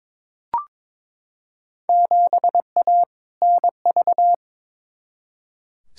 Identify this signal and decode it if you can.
Morse code, heard as a keyed tone, about 22 wpm, 700 Hz. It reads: 7A NV
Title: Field Day Exchanges Encore - 22wpm